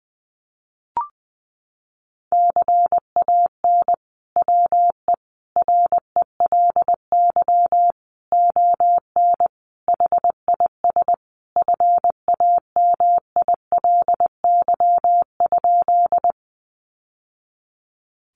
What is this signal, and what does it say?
Morse code, heard as a keyed tone, about 20 wpm, 700 Hz. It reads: CAN WE RELY ON HIS FAMILY?
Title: Sentences from Top 1200 Words 20wpm